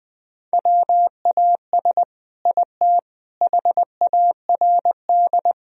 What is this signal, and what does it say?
Morse code, heard as a keyed tone, about 20 wpm, 700 Hz. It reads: WAS IT HARD
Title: Sentences from Top 1000 Words - Review Part 4 - 20wpm